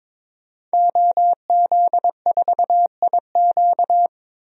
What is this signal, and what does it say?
Morse code, heard as a keyed tone, about 22 wpm, 700 Hz. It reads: OZ4IQ
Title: DXCC-100 Call Signs - 5 Characters - 22wpm